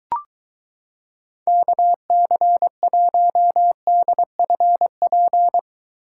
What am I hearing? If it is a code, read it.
Morse code, heard as a keyed tone, about 23 wpm, 700 Hz. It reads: KC1DFP